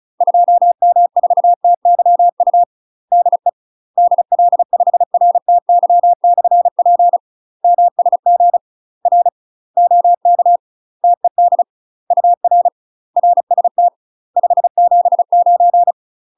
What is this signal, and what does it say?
Morse code, heard as a keyed tone, about 34 wpm, 700 Hz. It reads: 2M4TYU DE DL5RTY/P MSG R OK TED UR RST 579
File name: UK MASTER QSOs TRAINING / 35WPM / UK MASTER QSOs 0025